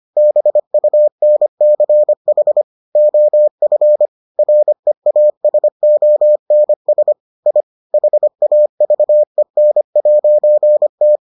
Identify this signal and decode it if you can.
Morse code, heard as a keyed tone, about 25 wpm, 600 Hz. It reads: BUNCH OF REASONS I HAVEN'T